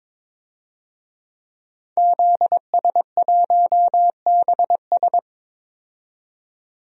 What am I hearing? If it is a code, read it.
Morse code, heard as a keyed tone, about 21 wpm, 700 Hz. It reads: ZS1BS
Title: DXCC-100 Call Signs - 5 Characters - 22wpm